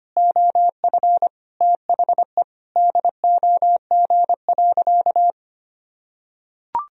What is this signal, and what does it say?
Morse code, heard as a keyed tone, about 25 wpm, 700 Hz. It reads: OF THE DOG.
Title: Sentences from Top 1000 Words - Review Part 2 - 25wpm